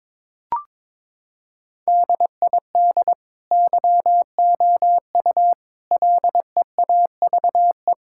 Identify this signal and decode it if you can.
Morse code, heard as a keyed tone, about 22 wpm, 700 Hz. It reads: DID YOU LEAVE
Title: Sentences from Top 1600 Words 22wpm